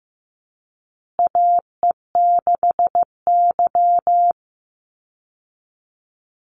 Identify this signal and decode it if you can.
Morse code, heard as a keyed tone, about 14 wpm, 700 Hz. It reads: AE6Y
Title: Call Signs Repeated 2x - 15wpm